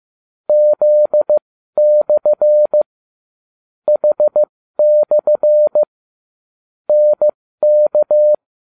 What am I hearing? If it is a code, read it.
Morse code, heard as a keyed tone, about 15 wpm, spaced as for 13 wpm, 600 Hz. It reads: Z/ H/ NK